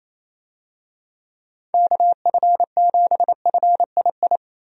Morse code, heard as a keyed tone, about 28 wpm, 700 Hz. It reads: KF7FII
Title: USA Call Signs - 6 Characters - Repeated 2x - 28wpm